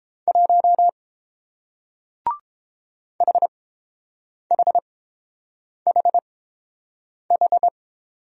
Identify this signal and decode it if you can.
Morse code, heard as a keyed tone, about 30 wpm, spaced as for 7 wpm, 700 Hz. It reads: 1 HHHH